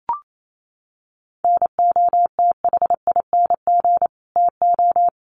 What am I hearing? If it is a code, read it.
Morse code, heard as a keyed tone, about 28 wpm, 700 Hz. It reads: NOTHING TO
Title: Sentences from Top 1300 Words 28wpm